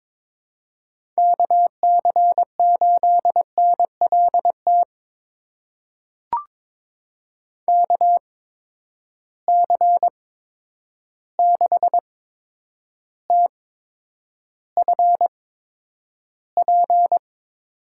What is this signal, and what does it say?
Morse code, heard as a keyed tone, about 22 wpm, spaced as for 9 wpm, 700 Hz. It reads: KC8NLT KC6TFP
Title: USA Call Signs - 6 Characters - 8x Character Spacing - 22wpm